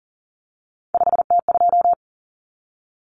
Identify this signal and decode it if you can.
Morse code, heard as a keyed tone, about 40 wpm, 700 Hz. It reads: 5T2
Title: Common Cut Numbers - 3-Digit 40wpm